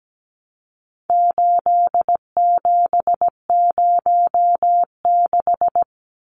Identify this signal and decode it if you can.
Morse code, heard as a keyed tone, about 17 wpm, 700 Hz. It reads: 8706
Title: Numbers - 4-Digit 17wpm